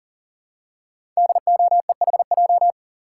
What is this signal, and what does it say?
Morse code, heard as a keyed tone, about 42 wpm, 690 Hz. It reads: DOEHJ
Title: Code Groups - 5 Letters - 40wpm